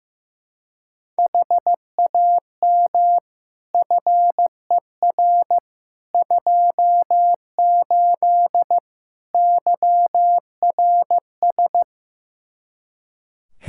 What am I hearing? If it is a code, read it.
Morse code, heard as a keyed tone, about 15 wpm, 700 Hz. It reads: HAM FER 28 YRS
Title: Practice QSO - Exchange 3 - More Info - 15wpm